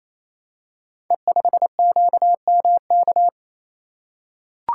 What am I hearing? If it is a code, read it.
Morse code, heard as a keyed tone, about 29 wpm, 700 Hz. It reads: E5QMK
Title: Code Groups - 5 Letter:Numbers - 28wpm